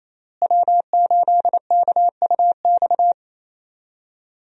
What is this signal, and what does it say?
Morse code, heard as a keyed tone, about 27 wpm, 700 Hz. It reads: W8KUX
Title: USA Call Signs - 5 Characters - Repeated 2x - 28wpm